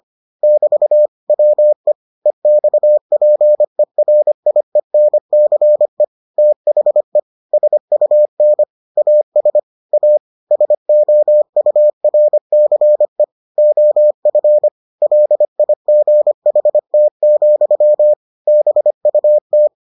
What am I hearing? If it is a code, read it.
Morse code, heard as a keyed tone, about 25 wpm, 600 Hz. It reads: <BT> WE EXPERIENCE THE SUN AS A SOURCE OF LIGHT, BUT